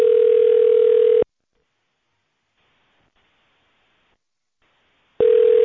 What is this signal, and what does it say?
Ringback tone (North America), 440 + 480 Hz.